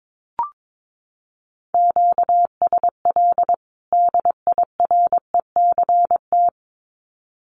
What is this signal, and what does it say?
Morse code, heard as a keyed tone, about 22 wpm, 700 Hz. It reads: QSL DIRECT
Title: Common QSO Phrases 22wpm